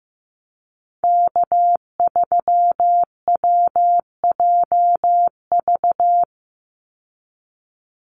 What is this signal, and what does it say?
Morse code, heard as a keyed tone, about 15 wpm, 700 Hz. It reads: K3WJV
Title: Call Signs Repeated 3x - 15wpm